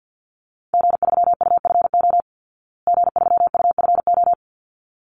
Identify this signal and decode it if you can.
Morse code, heard as a keyed tone, about 50 wpm, 700 Hz. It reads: G3UFO G3UFO
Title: Call Signs Repeated 3x - 50wpm